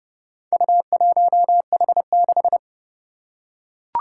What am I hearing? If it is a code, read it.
Morse code, heard as a keyed tone, about 30 wpm, 700 Hz. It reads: U1H6
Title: Four Challenging Letter-Number Combinations 30wpm